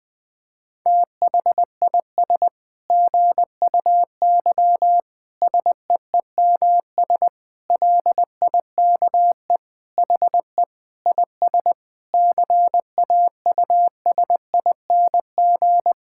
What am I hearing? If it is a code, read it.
Morse code, heard as a keyed tone, about 20 wpm, 700 Hz. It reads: THIS GUY SEEMS LIKE HE IS CAUSING
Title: Sentences with 9-Word Phrases - Spoken Beginning - 20wpm